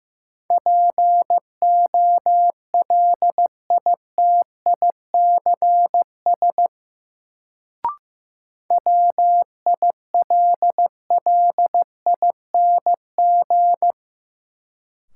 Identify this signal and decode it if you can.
Morse code, heard as a keyed tone, about 15 wpm, 700 Hz. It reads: POLITICS WILLING
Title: Sentences from Top 1000 Words - New Word Drill - 15wpm